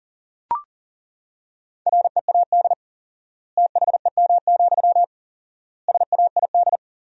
Morse code, heard as a keyed tone, about 40 wpm, 700 Hz. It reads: READ THEM, SAID
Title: 4x Word-Spacing - Sentences from Top 400 Words - 40wpm